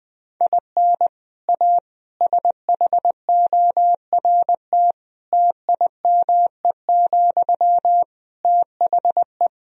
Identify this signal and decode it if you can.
Morse code, heard as a keyed tone, about 21 wpm, 700 Hz. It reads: IN A SHORT TIME, THE
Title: Sentences from Top 1000 Words - Review Part 2 - 20wpm